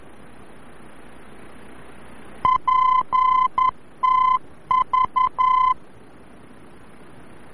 Morse code, heard as a keyed tone, about 11 wpm, 1020 Hz. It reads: PTV
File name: france / mx / PTV-VOR-F